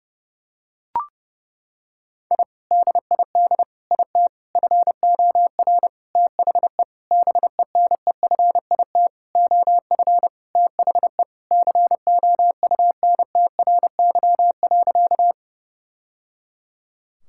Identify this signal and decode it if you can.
Morse code, heard as a keyed tone, about 30 wpm, 700 Hz. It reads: I DID IT FOR THE BENEFIT OF THE COUNTRY.
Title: Sentences from Top 1000 Words - Review Part 6 - 30wpm